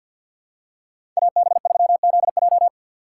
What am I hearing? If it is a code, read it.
Morse code, heard as a keyed tone, about 51 wpm, 700 Hz. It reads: AB3ZJ
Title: USA Call Signs - 5 Characters - 4x Character Spacing - 50wpm